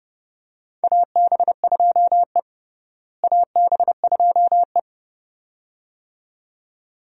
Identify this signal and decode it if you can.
Morse code, heard as a keyed tone, about 30 wpm, 700 Hz. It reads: AB2E AB2E
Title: USA Call Signs - 4 Characters - Repeated 2x - 30wpm